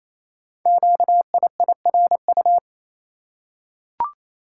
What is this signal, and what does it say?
Morse code, heard as a keyed tone, about 27 wpm, 700 Hz. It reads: QIIRU
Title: Code Groups - 5 Letters - 28wpm